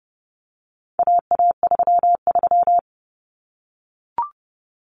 Morse code, heard as a keyed tone, about 30 wpm, 700 Hz. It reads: AA33
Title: Common Cut Numbers - 4-Digit 30wpm